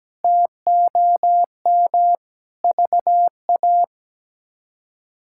Morse code, heard as a keyed tone, about 17 wpm, 700 Hz. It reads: TOM VA
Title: K1USN SST QSO Practice - 17wpm